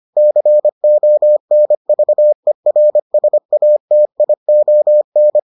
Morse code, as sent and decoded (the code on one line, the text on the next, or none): -.-. --- -. ...- . .-. ... .- - .. --- -.
CONVERSATION